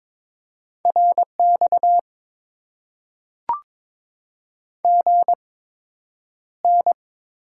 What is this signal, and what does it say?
Morse code, heard as a keyed tone, about 22 wpm, spaced as for 9 wpm, 700 Hz. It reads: RX GN
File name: Top 90 QSO Elements - 8x Character Spacing - 22wpm